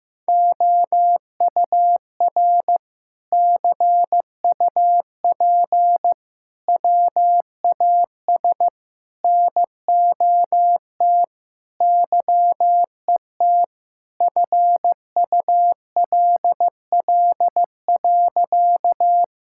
Morse code, heard as a keyed tone, about 15 wpm, 700 Hz. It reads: OUR CUP WAS NOT YET FULL.
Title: Sentences from Top 1600 Words 15wpm